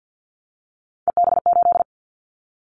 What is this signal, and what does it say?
Morse code, heard as a keyed tone, about 50 wpm, 700 Hz. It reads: EB8
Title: Callsign Prefixes for Top 50 DXCC - 50wpm